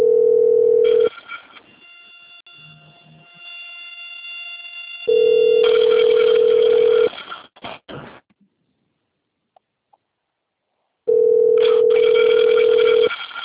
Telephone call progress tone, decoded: ringback tone

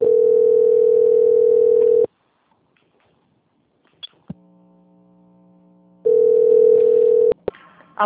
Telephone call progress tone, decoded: ringback tone

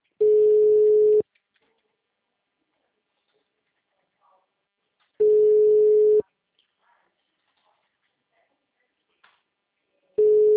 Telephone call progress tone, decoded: ringback tone

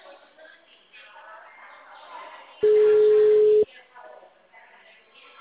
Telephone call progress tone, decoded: ringback tone

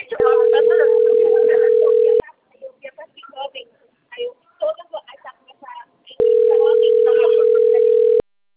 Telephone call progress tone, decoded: ringback tone

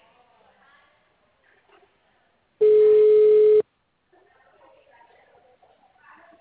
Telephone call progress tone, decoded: ringback tone

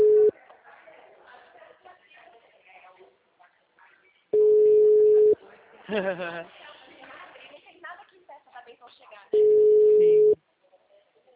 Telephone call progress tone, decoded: ringback tone

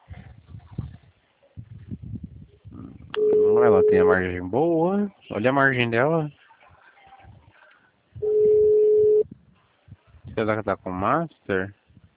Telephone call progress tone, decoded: ringback tone

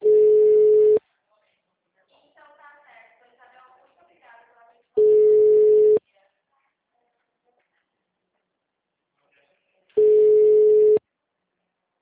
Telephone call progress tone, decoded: ringback tone

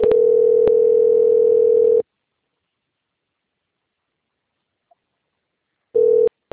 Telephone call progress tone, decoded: ringback tone

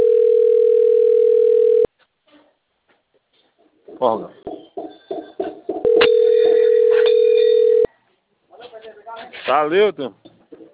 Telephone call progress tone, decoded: ringback tone

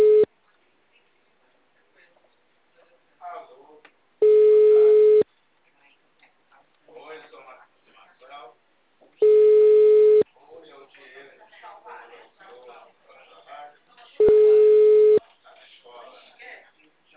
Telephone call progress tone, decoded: ringback tone

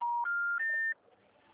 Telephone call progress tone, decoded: special information tone

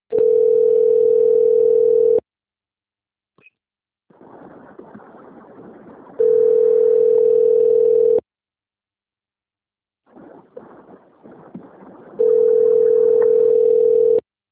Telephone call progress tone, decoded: ringback tone

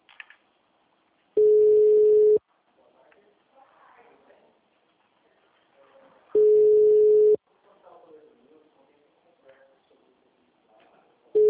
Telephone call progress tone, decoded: ringback tone

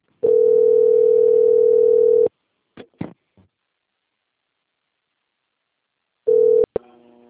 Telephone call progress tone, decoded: ringback tone

